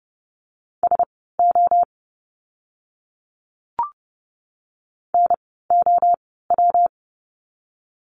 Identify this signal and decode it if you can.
Morse code, heard as a keyed tone, about 30 wpm, spaced as for 17 wpm, 700 Hz. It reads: SO NOW
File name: course / 30:10wpm / 230 Lesson 37f - Letter x - Words 30wpm